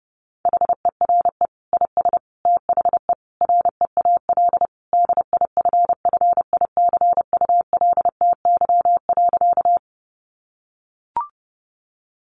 Morse code, heard as a keyed tone, about 30 wpm, 700 Hz. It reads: HERE IS THE REAL DIFFICULTY.
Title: Sentences from Top 1500 Words 30wpm